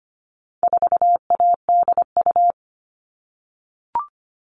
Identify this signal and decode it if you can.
Morse code, heard as a keyed tone, about 25 wpm, 700 Hz. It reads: 4ADU
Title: All Cut Numbers - 4-Digit 25wpm